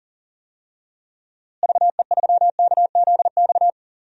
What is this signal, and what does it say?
Morse code, heard as a keyed tone, about 40 wpm, 700 Hz. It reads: VE3KZX
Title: DXCC-100 Call Signs - 6 Characters - Repeated 3x - 40wpm